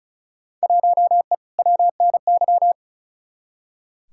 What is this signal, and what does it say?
Morse code, heard as a keyed tone, about 36 wpm, 700 Hz. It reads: 1E WNY
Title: Field Day Exchanges - 35wpm